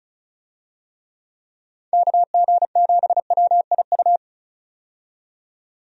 Morse code, heard as a keyed tone, about 36 wpm, 700 Hz. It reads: KG7WIU
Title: USA Call Signs - 6 Characters - 35wpm